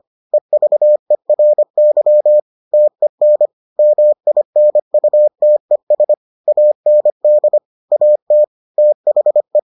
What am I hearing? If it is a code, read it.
Morse code, heard as a keyed tone, about 25 wpm, 600 Hz. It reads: EVERY TEN MINUTES AND AT THE